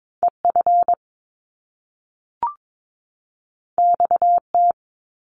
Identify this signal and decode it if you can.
Morse code, heard as a keyed tone, about 22 wpm, 700 Hz. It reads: EF XT
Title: Top Two Letter Combinations in Words 22wpm